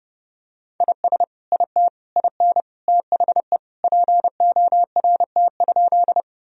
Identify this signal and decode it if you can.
Morse code, heard as a keyed tone, about 30 wpm, 700 Hz. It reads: IS IT IN THE PORT?
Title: Sentences from Top 1000 Words - Review Part 1 - 30wpm